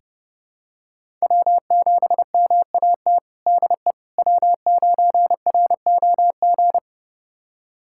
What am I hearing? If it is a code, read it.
Morse code, heard as a keyed tone, about 30 wpm, 700 Hz. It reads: W7MAT DE W9ROG
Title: USA Callsign DE Callsign - 5 Characters - 30wpm